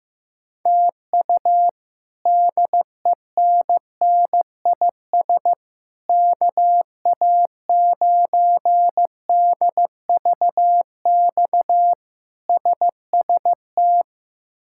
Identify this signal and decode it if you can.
Morse code, heard as a keyed tone, about 16 wpm, 700 Hz. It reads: TU DENNIS KA9DVX SST
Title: K1USN SST QSO Practice - 15wpm